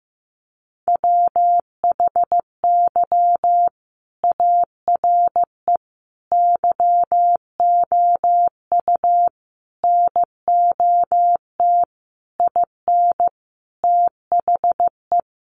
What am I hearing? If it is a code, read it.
Morse code, heard as a keyed tone, about 15 wpm, 700 Hz. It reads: WHY ARE YOU NOT IN THE
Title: Sentences from Top 1400 Words 15wpm